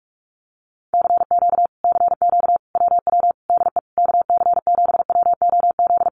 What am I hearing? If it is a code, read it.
Morse code, heard as a keyed tone, about 46 wpm, 700 Hz. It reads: CQ CQ WW DE KC7WOZ
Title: Common QSO Phrases Encore 45wpm